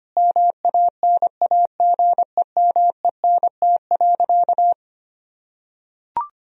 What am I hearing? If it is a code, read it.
Morse code, heard as a keyed tone, about 25 wpm, 700 Hz. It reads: MANAGEMENT.
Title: Sentences from Top 1000 Words - Review Part 4 - 25wpm